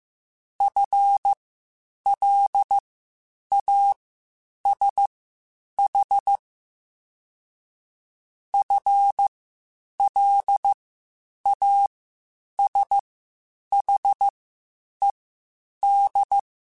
Morse code, heard as a keyed tone, about 15 wpm, spaced as for 8 wpm, 800 Hz. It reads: FLASH FLASHED